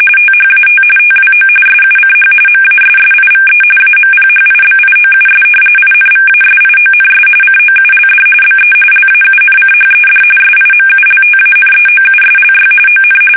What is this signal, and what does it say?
RTTY, about 45.45 baud. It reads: THAT TWO SUCH MASSIVE AMPHIBIOUS INVASIONS COULD BE LAUNCHED SIMULTANEOUSLY ON
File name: fsk850 / battleshipnj